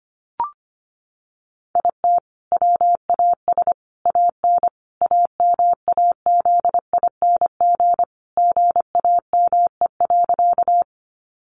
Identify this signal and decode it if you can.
Morse code, heard as a keyed tone, about 25 wpm, 700 Hz. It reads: IT WAS AN AMAZING GAME.
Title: Sentences from Top 1500 Words 25wpm